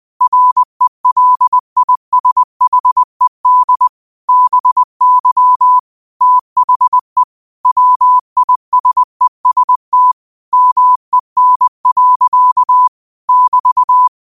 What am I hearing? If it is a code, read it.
Morse code, heard as a keyed tone, about 20 wpm, 1000 Hz. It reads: RELISHED BY THE WISEST MEN. <BT>
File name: qotdfiles / 20wpm